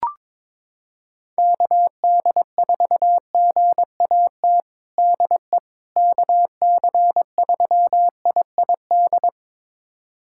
Morse code, heard as a keyed tone, about 22 wpm, 700 Hz. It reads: KD4GAT DE KC3IID
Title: USA Callsign DE Callsign - 6 Characters - 22wpm